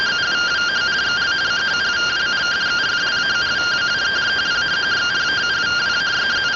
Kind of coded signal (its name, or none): two-tone teleprinter signal